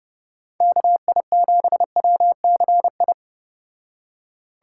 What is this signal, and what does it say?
Morse code, heard as a keyed tone, about 29 wpm, 700 Hz. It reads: KI7WCI
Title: Speed Racing - USA 2x3 Call Signs 25wpm